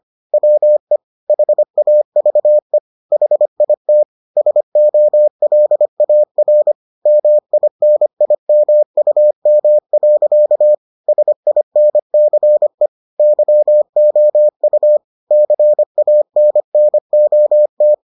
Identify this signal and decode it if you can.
Morse code, heard as a keyed tone, about 25 wpm, 600 Hz. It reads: WE HAVE HIT SOLAR MINIMUM. SINCE YOU CANNOT